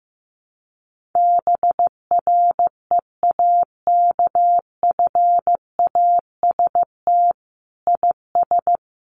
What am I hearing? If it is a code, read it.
Morse code, heard as a keyed tone, about 15 wpm, 700 Hz. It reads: BREAKFAST IS
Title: Sentences from Top 1500 Words 15wpm